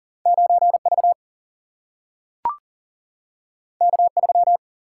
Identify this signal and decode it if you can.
Morse code, heard as a keyed tone, about 39 wpm, 700 Hz. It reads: 9 V K 3
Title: Single-Letter Single-Number 40wpm